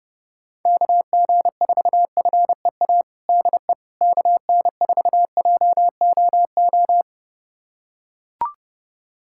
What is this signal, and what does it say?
Morse code, heard as a keyed tone, about 32 wpm, 700 Hz. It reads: KG4FEA DE KN4JOO
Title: USA Callsign DE Callsign - 6 Characters - 30wpm